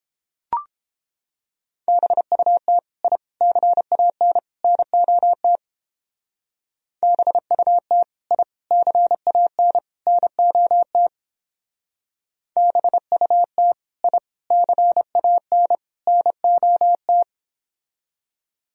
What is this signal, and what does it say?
Morse code, heard as a keyed tone, about 29 wpm, 700 Hz. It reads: BUT I CAN NOT BUT I CAN NOT BUT I CAN NOT